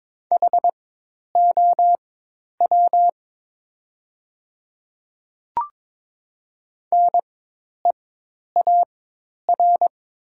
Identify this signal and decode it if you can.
Morse code, heard as a keyed tone, about 22 wpm, spaced as for 10 wpm, 700 Hz. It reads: HOW NEAR
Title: 4x Character Spacing - Top 200 Words - 22wpm